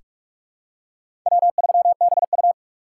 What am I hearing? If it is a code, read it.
Morse code, heard as a keyed tone, about 41 wpm, 700 Hz. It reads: W3BU
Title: USA Call Signs - 4 Characters - 6x Character Spacing - 45wpm